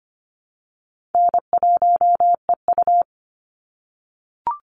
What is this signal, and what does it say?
Morse code, heard as a keyed tone, about 25 wpm, 700 Hz. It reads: N1EU
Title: NA SOTA Chaser Callsigns 25wpm